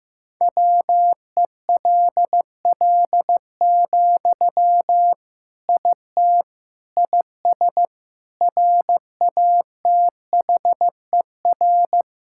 Morse code, heard as a keyed tone, about 15 wpm, 700 Hz. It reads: WELL, IT IS RATHER
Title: Sentences from Top 1600 Words 15wpm